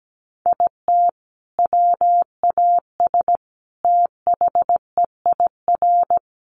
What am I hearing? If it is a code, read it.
Morse code, heard as a keyed tone, about 17 wpm, 700 Hz. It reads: IT WAS THEIR
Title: Sentences from Top 1500 Words 17wpm